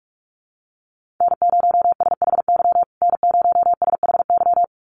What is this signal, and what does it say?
Morse code, heard as a keyed tone, about 44 wpm, 700 Hz. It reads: N0SHY N0SHY
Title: USA Call Signs Repeated 2x 45wpm